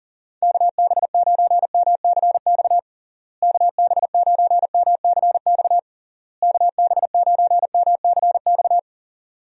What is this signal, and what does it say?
Morse code, heard as a keyed tone, about 40 wpm, 700 Hz. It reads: KB9MCX KB9MCX KB9MCX
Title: USA Call Signs - 6 Characters - Repeated 3x - 40wpm